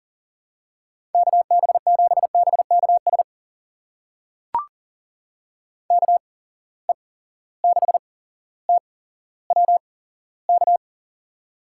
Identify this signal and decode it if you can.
Morse code, heard as a keyed tone, about 40 wpm, spaced as for 17 wpm, 700 Hz. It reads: KB7BKS KE6TWK
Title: USA Call Signs - 8x Character Spacing - 40wpm